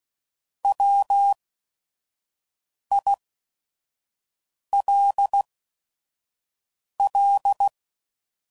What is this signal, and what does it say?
Morse code, heard as a keyed tone, about 16 wpm, spaced as for 5 wpm, 800 Hz. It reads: WILL
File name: Prefixes / By First Letter / Ws